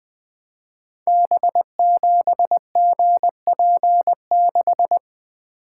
Morse code, heard as a keyed tone, about 21 wpm, 700 Hz. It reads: B7GP6
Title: Code Groups - 5 Letter:Numbers - 20wpm